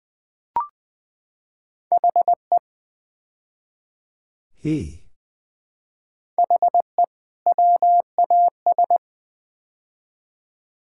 Morse code, heard as a keyed tone, about 21 wpm, 700 Hz. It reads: HE HE WAS
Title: Build A Phrase - 5-Word Phrases - 20wpm